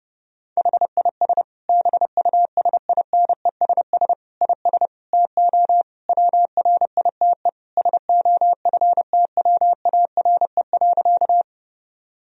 Morse code, heard as a keyed tone, about 32 wpm, 700 Hz. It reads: HIS BUSINESS IS TO WRITE SOFTWARE.
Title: Sentences from Top 1000 Words - Review Part 3 - 30wpm